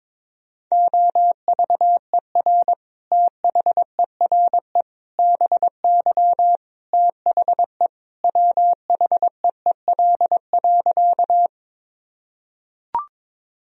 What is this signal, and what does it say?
Morse code, heard as a keyed tone, about 22 wpm, 700 Hz. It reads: OVER THERE BY THE WHEEL.
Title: Sentences from Top 1000 Words - Review Part 3 - 22wpm